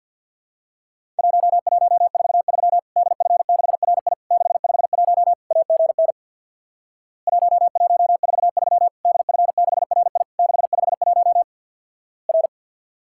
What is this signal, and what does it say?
Morse code, heard as a keyed tone, about 50 wpm, 700 Hz. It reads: 1143 DF6RI 651 AGN 1143 DF6RI 651 R
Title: Worked All Europe DX Contest - QTC Exchange with Repeats - 50wpm